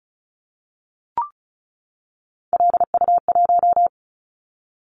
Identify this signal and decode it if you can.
Morse code, heard as a keyed tone, about 34 wpm, 700 Hz. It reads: LU1
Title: Callsign Prefixes for Top 100 DXCC - 35wpm